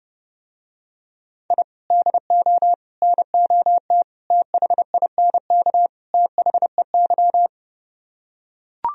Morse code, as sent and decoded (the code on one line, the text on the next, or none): .. / -.. --- / -. --- - / - .... .. -. -.- / - .... -.--
I DO NOT THINK THEY